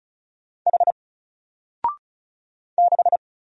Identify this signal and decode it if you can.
Morse code, heard as a keyed tone, about 36 wpm, 700 Hz. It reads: H 6
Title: Single Letter-Number Alt 35wpm